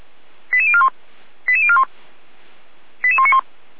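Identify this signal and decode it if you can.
Five-tone selective call (ZVEI), heard as quick runs of five tones, tones of about 70 ms. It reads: 88041 88041 88191
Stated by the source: ZVEI five-tone call 88076 received by SDR.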